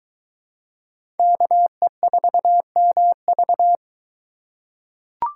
Morse code, heard as a keyed tone, about 23 wpm, 700 Hz. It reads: KE4MV